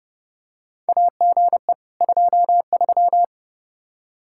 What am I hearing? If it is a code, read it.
Morse code, heard as a keyed tone, about 30 wpm, 700 Hz. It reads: AGE 23
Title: Practice QSO - Exchange 3 - More Info - 30wpm